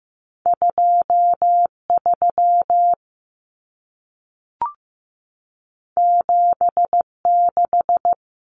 Morse code, heard as a keyed tone, about 15 wpm, 700 Hz. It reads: 23 76
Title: Numbers - 2-Digit 15wpm